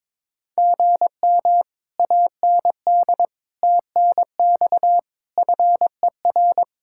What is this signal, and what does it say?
Morse code, heard as a keyed tone, about 22 wpm, 700 Hz. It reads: GM AND TNX FER
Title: Speed-Racing Common QSO Phrases Encore - 22wpm